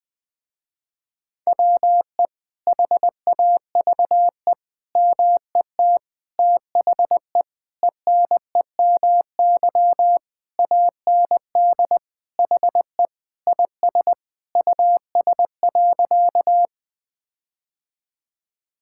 Morse code, heard as a keyed tone, about 21 wpm, 700 Hz. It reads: WE HAVE MET THE ENEMY AND HE IS US.
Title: Sentences with 9-Word Phrases - Spoken Beginning - 20wpm